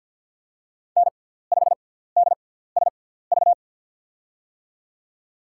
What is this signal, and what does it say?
Morse code, heard as a keyed tone, about 53 wpm, spaced as for 17 wpm, 700 Hz. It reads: N5DSV
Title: USA Call Signs - 5 Characters - 6x Character Spacing - 50wpm